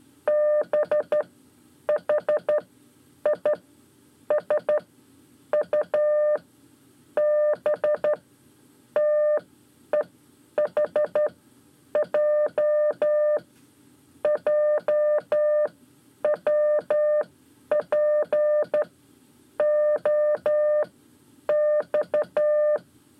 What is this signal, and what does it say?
Morse code, heard as a keyed tone, about 12 wpm, spaced as for 8 wpm, 600 Hz. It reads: BHISU BTEHJ JWPOX
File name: 5 Lezione CW 1